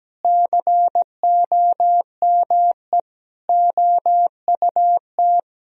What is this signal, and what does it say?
Morse code, heard as a keyed tone, about 17 wpm, 700 Hz. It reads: COME OUT
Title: Sentences from Top 1000 Words - Review Part 3 - 17wpm